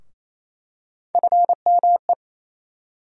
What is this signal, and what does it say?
Morse code, heard as a keyed tone, about 28 wpm, 700 Hz. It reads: FME
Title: USA Call Sign Postfixes - 3 Letters 28wpm